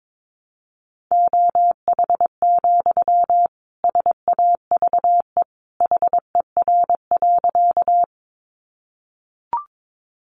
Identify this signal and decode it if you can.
Morse code, heard as a keyed tone, about 22 wpm, 700 Hz. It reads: OH, SAVE HER.
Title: Sentences from Top 1100 Words 22wpm